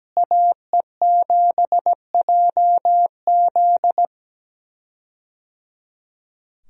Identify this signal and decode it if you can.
Morse code, heard as a keyed tone, about 17 wpm, 700 Hz. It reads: AE7JZ
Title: USA Call Signs - 5 Characters - Repeated 3x - 17wpm